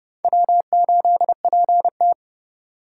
Morse code, heard as a keyed tone, about 30 wpm, 700 Hz. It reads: W8PT
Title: USA Call Signs - 4 Characters - 3x Character Spacing - 30wpm